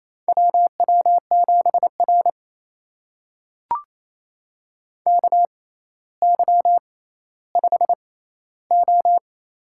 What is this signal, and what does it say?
Morse code, heard as a keyed tone, about 28 wpm, spaced as for 14 wpm, 700 Hz. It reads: WW7R KY5O
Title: USA Call Signs - 4 Characters - 6x Character Spacing - 28wpm